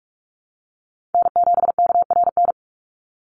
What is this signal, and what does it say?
Morse code, heard as a keyed tone, about 44 wpm, 700 Hz. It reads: N7KRN
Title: NA SOTA Chaser Callsigns 45wpm